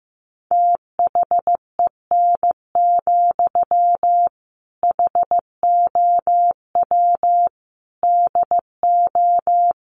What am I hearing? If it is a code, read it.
Morse code, heard as a keyed tone, about 15 wpm, 700 Hz. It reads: THEN, HOW DO
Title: Sentences from Top 100 Words 15wpm